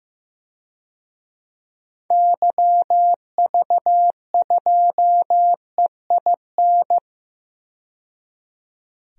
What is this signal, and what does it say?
Morse code, heard as a keyed tone, about 15 wpm, 700 Hz. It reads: YV2EIN
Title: DXCC-100 Call Signs - 6 Characters - 15wpm